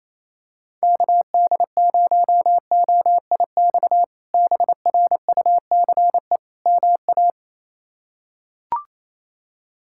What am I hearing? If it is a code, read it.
Morse code, heard as a keyed tone, about 28 wpm, 700 Hz. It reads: KD0OIX BRUCE MA
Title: K1USN SST QSO Practice - 28wpm